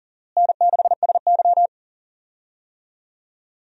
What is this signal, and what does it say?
Morse code, heard as a keyed tone, about 40 wpm, 700 Hz. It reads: N6SY
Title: USA Call Signs - 4 Characters - Repeated 3x - 40wpm